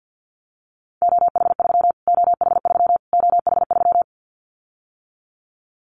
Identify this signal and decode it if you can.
Morse code, heard as a keyed tone, about 50 wpm, 700 Hz. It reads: OH3 OH3 OH3
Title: Callsign Prefixes for Top 25 DXCC - Repeated 3x - 50wpm